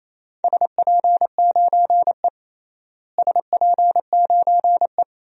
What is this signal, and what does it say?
Morse code, heard as a keyed tone, about 28 wpm, 700 Hz. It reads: SP9E SP9E
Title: DXCC-100 Call Signs - 4 Characters - Repeated 3x - 28wpm